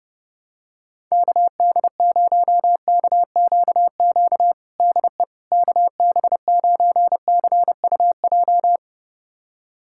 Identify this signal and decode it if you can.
Morse code, heard as a keyed tone, about 32 wpm, 700 Hz. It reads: KD0KQQ DE KB9CUJ
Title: USA Callsign DE Callsign - 6 Characters - 30wpm